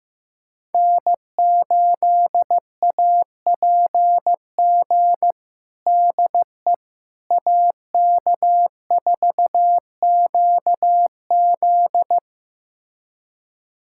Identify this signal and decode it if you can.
Morse code, heard as a keyed tone, about 15 wpm, 700 Hz. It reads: N8APG DE AK4QZ
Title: USA Callsign DE Callsign - 5 Characters - 15wpm